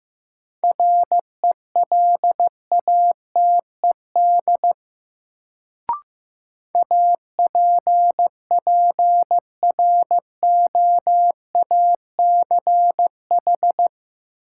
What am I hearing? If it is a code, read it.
Morse code, heard as a keyed tone, about 15 wpm, 700 Hz. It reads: RELATED APPROACH
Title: Sentences from Top 1400 Words - New Word Drill Alt - 15wpm